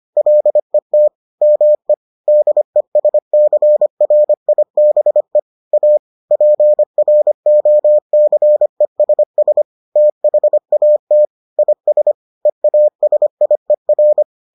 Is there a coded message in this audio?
Yes: Morse code from 0.2 to 14.2 s, heard as a keyed tone, about 25 wpm, 600 Hz. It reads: LET ME DESCRIBE A PROCESS THAT IS EASIER